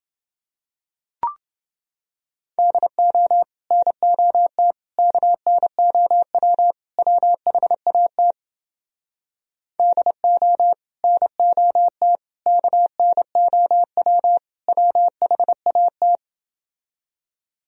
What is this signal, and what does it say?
Morse code, heard as a keyed tone, about 30 wpm, 700 Hz. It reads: DO NOT KNOW WHAT DO NOT KNOW WHAT